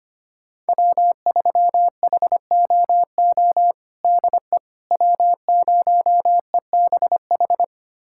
Morse code, heard as a keyed tone, about 25 wpm, 700 Hz. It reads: W3HOO DE W0EBH
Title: USA Callsign DE Callsign - 5 Characters - 25wpm